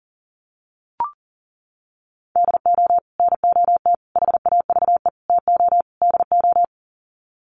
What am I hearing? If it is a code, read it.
Morse code, heard as a keyed tone, about 40 wpm, 700 Hz. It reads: DO NOT HAVE TO DO
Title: Sets of 5 Words 40wpm